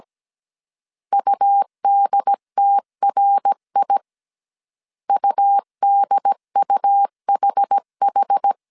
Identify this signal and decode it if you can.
Morse code, heard as a keyed tone, about 17 wpm, 800 Hz. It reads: UDTRI UDUHH
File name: CW / DET / 1 3 1500C S A 16wpm 2141